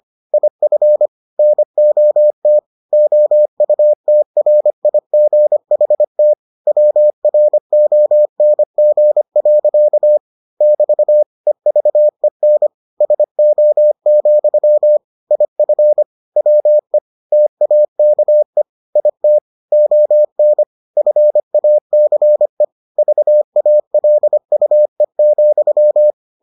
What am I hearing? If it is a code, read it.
Morse code, heard as a keyed tone, about 25 wpm, 600 Hz. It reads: IF NOT OUTRIGHT WRONG. <BT> EVEN SO, IF WE TAKE IT ON FACE VALUE,